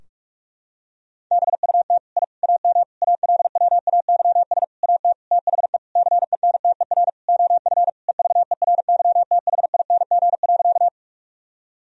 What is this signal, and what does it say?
Morse code, heard as a keyed tone, about 44 wpm, 700 Hz. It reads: BUT I AM ALWAYS AT THE CENTER OF EVERYTHING.
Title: Sentences from Top 1000 Words - Review Part 2 - 45wpm